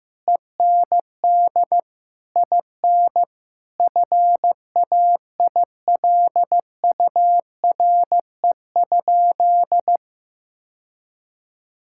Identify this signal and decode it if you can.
Morse code, heard as a keyed tone, about 15 wpm, 700 Hz. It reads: END IN FAILURE?
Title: Sentences from Top 1000 Words - Review Part 6 - 15wpm